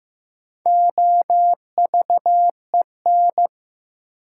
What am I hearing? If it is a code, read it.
Morse code, heard as a keyed tone, about 15 wpm, 700 Hz. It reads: OVEN
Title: Sentences from Top 1000 Words - Review Part 4 - New Word Drill - 15wpm